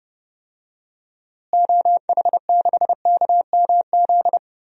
Morse code, heard as a keyed tone, about 30 wpm, 700 Hz. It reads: OH6KMZ
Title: DXCC-100 Call Signs - 6 Characters - Repeated 3x - 30wpm